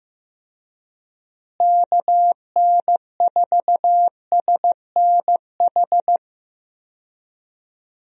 Morse code, heard as a keyed tone, about 16 wpm, 700 Hz. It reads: KN4SNH
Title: USA Call Signs - 6 Characters - Repeated 2x - 15wpm